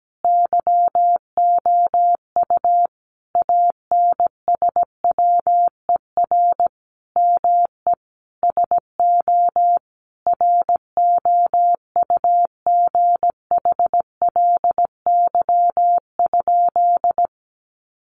Morse code, heard as a keyed tone, about 17 wpm, 700 Hz. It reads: YOU ANSWER ME SO ROUGHLY?
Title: Sentences from Top 1500 Words 17wpm